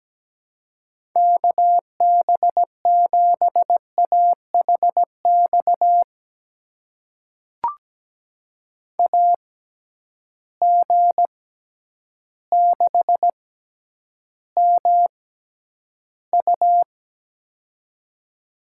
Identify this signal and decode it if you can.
Morse code, heard as a keyed tone, about 17 wpm, spaced as for 9 wpm, 700 Hz. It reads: KB7AHX AG6MU
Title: USA Call Signs - 6x Character Spacing - 17wpm